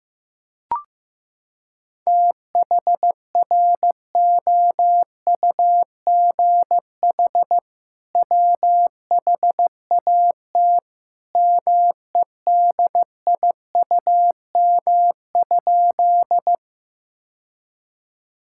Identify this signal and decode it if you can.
Morse code, heard as a keyed tone, about 15 wpm, 700 Hz. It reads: THROUGH WHAT MEDIUM?
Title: Sentences from Top 1000 Words - Review Part 6 - 15wpm